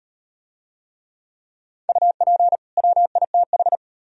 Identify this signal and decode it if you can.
Morse code, heard as a keyed tone, about 37 wpm, 700 Hz. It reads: UP WITH